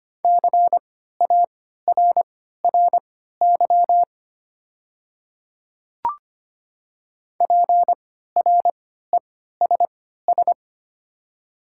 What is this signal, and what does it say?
Morse code, heard as a keyed tone, about 25 wpm, spaced as for 14 wpm, 700 Hz. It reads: CARRY PRESS
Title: 3x Character Spacing - Top 300 Words - 25wpm